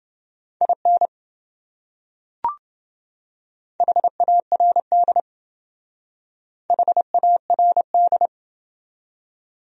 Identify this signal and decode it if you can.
Morse code, heard as a keyed tone, about 27 wpm, 700 Hz. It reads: IN HARD HARD